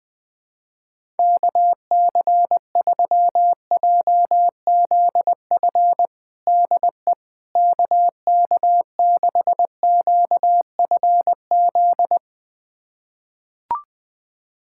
Morse code, heard as a keyed tone, about 20 wpm, 700 Hz. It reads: KC3JZF DE KK6QFZ
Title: USA Callsign DE Callsign 20wpm